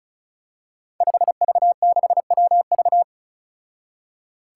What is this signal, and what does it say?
Morse code, heard as a keyed tone, about 36 wpm, 700 Hz. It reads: 5V6WV
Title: Five Challenging Letter-Number Combinations 35wpm